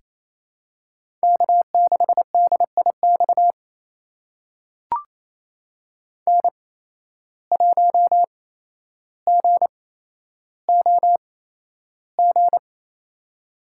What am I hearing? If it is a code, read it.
Morse code, heard as a keyed tone, about 28 wpm, spaced as for 12 wpm, 700 Hz. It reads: K6DIX N1GOG
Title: USA Call Signs - 5 Characters - 8x Character Spacing - 28wpm